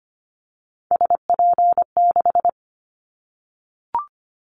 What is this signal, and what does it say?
Morse code, heard as a keyed tone, about 25 wpm, 700 Hz. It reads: SP6
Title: Callsign Prefixes for Top 25 DXCC - Repeated 3x - 25wpm